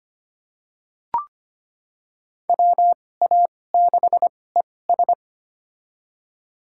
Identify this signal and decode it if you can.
Morse code, heard as a keyed tone, about 25 wpm, spaced as for 18 wpm, 700 Hz. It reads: WA6ES
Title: USA Call Signs - 5 Characters - 2x Character Spacing - 25wpm